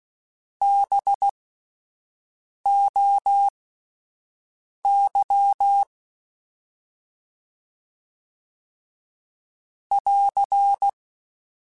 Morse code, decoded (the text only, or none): BOY <AR>